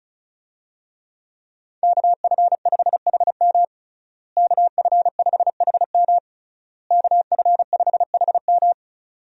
Morse code, heard as a keyed tone, about 34 wpm, 700 Hz. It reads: KF5HM KF5HM KF5HM
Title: USA Call Signs - 5 Characters - Repeated 3x - 35wpm